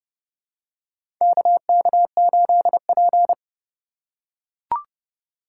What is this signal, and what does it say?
Morse code, heard as a keyed tone, about 31 wpm, 700 Hz. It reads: KK8P
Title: USA Call Signs - 4 Characters - 3x Character Spacing - 30wpm